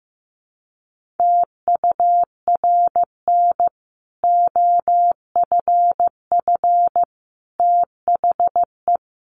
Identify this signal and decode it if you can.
Morse code, heard as a keyed tone, about 15 wpm, 700 Hz. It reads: TURN OFF THE
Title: Sentences from Top 1400 Words 15wpm